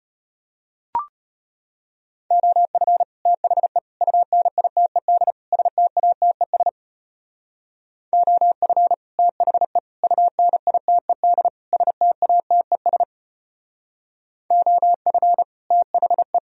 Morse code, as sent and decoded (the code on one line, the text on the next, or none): --- ..-. / - .... / ..- -. .. - . -.. / ... - .- - . ... / --- ..-. / - .... / ..- -. .. - . -.. / ... - .- - . ... / --- ..-. / - .... .
OF THE UNITED STATES OF THE UNITED STATES OF THE